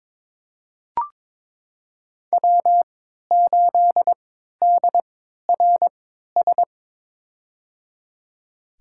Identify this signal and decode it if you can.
Morse code, heard as a keyed tone, about 22 wpm, spaced as for 12 wpm, 700 Hz. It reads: W8DRS
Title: USA Call Signs - 5 Characters - 3x Character Spacing - 22wpm